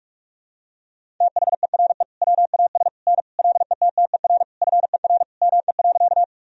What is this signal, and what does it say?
Morse code, heard as a keyed tone, about 47 wpm, 700 Hz. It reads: THERE WAS N LETTER FER ME.
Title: Sentences from Top 300 Words and Abbreviations 45wpm